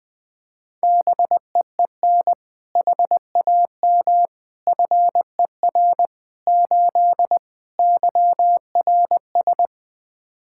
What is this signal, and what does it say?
Morse code, heard as a keyed tone, about 20 wpm, 700 Hz. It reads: BEEN HAM FER 8 YRS
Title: Practice QSO - Exchange 3 - More Info - 20wpm